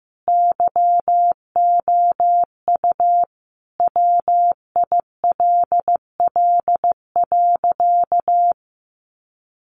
Morse code, heard as a keyed tone, about 15 wpm, 700 Hz. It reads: YOU WILL.
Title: Sentences from Top 200 Words 15wpm